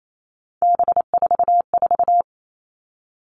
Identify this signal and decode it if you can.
Morse code, heard as a keyed tone, about 28 wpm, 700 Hz. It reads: B44
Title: All Cut Numbers - 3-Digit 28wpm